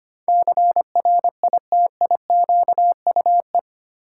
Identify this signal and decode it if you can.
Morse code, heard as a keyed tone, about 25 wpm, 700 Hz. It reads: CRITIQUE
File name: Eight Letter Words - 25wpm